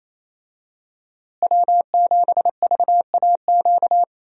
Morse code, heard as a keyed tone, about 28 wpm, 700 Hz. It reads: W7VAQ
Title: USA Call Signs - 5 Characters - Repeated 2x - 28wpm